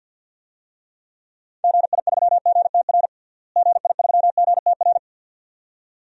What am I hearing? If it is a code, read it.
Morse code, heard as a keyed tone, about 51 wpm, 700 Hz. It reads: GI3GTR GI3GTR
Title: DXCC-100 Call Signs - 6 Characters - Repeated 2x - 50wpm